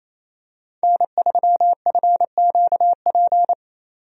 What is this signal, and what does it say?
Morse code, heard as a keyed tone, about 27 wpm, 700 Hz. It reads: N3FQP
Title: USA Call Signs - 5 Characters - 28wpm